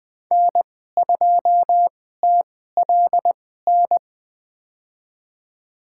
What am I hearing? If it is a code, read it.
Morse code, heard as a keyed tone, about 20 wpm, spaced as for 14 wpm, 700 Hz. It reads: N2TLN
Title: USA Call Signs - 2x Character Spacing - 20wpm